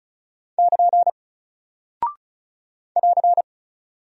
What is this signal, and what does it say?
Morse code, heard as a keyed tone, about 36 wpm, 700 Hz. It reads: <KN> <AR>
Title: Single Character (Letter:Number:Punctuation:Prosign) Alt 35wpm